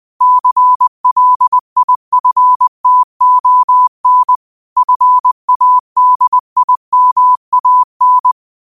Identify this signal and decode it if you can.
Morse code, heard as a keyed tone, about 20 wpm, 1000 Hz. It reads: CLIFTON FADIMAN
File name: qotdfiles / 20wpm